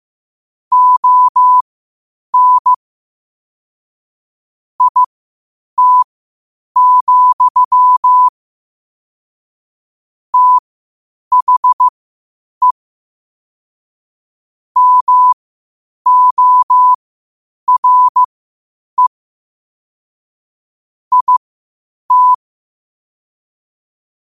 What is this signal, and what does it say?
Morse code, heard as a keyed tone, about 15 wpm, spaced as for 8 wpm, 1000 Hz. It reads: ON IT, THE MORE IT
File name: qotdfiles / 8wpm